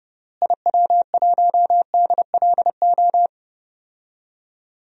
Morse code, heard as a keyed tone, about 32 wpm, 700 Hz. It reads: IW1DLO
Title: DXCC-100 Call Signs - 6 Characters - Repeated 2x - 30wpm